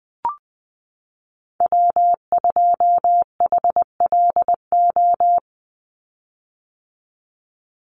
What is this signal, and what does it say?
Morse code, heard as a keyed tone, about 20 wpm, 700 Hz. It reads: W2HLO
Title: USA Call Signs 20wpm